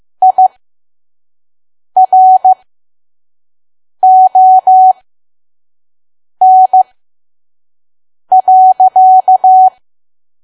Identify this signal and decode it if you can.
Morse code, heard as a keyed tone, about 15 wpm, spaced as for 5 wpm, 750 Hz. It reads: IRON.
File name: Archive / 5 WPM files / PM